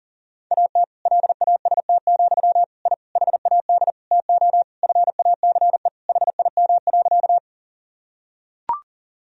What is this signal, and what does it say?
Morse code, heard as a keyed tone, about 40 wpm, 700 Hz. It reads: AT LAST, I HAD TO FACE HIM.
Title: Sentences from Top 1000 Words - Review Part 2 - 40wpm